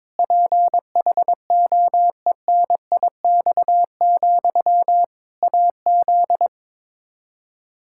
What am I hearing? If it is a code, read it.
Morse code, heard as a keyed tone, about 22 wpm, 700 Hz. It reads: PHOENIX, AZ
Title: Practice QSO - Exchange 1 - Beginning - 22wpm